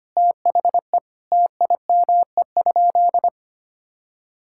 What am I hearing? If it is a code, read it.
Morse code, heard as a keyed tone, about 25 wpm, 700 Hz. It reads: THE TIME?
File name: practice / 25wpm / Sentences with 2-Word Phrases - 25wpm